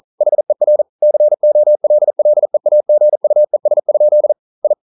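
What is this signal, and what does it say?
Morse code, heard as a keyed tone, about 42 wpm, 600 Hz. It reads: HER COLLEAGUES? I
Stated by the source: Morse code practice at 41 wpm, from CW med Gnister.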